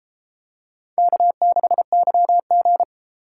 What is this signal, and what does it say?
Morse code, heard as a keyed tone, about 31 wpm, 700 Hz. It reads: K6YG